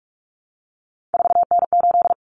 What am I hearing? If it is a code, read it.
Morse code, heard as a keyed tone, about 44 wpm, 700 Hz. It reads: 4N8
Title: Common Cut Numbers - 3-Digit 45wpm